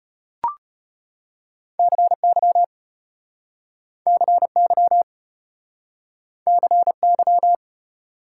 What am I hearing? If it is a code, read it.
Morse code, heard as a keyed tone, about 34 wpm, 700 Hz. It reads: CY CY CY